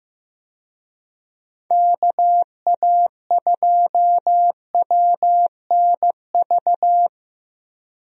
Morse code, heard as a keyed tone, about 15 wpm, 700 Hz. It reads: KA2WNV
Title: USA Call Signs - 6 Characters - 15wpm